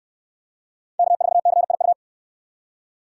Morse code, heard as a keyed tone, about 61 wpm, 700 Hz. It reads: K4CIU